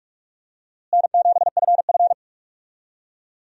Morse code, heard as a keyed tone, about 46 wpm, 700 Hz. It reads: N7FF
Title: USA Call Signs - 4 Characters - Repeated 2x - 45wpm